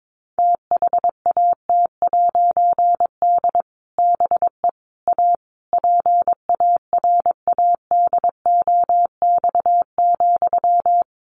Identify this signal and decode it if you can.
Morse code, heard as a keyed tone, about 22 wpm, 700 Hz. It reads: THAT'D BE A PARADOX,
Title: A Sound Of Thunder 22wpm